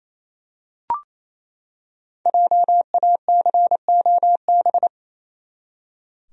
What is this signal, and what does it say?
Morse code, heard as a keyed tone, about 27 wpm, 700 Hz. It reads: JACOB
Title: USA Common Names - 28wpm